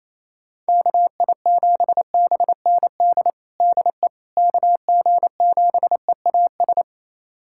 Morse code, heard as a keyed tone, about 28 wpm, 700 Hz. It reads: KI7BND DE KG7EAS
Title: USA Callsign DE Callsign - 6 Characters - 28wpm